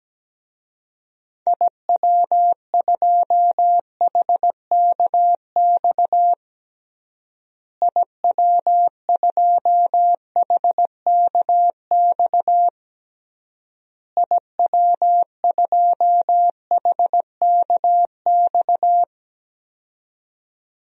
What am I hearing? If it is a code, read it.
Morse code, heard as a keyed tone, about 17 wpm, 700 Hz. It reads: IW2HKX IW2HKX IW2HKX
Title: DXCC-100 Call Signs - 6 Characters - Repeated 3x - 17wpm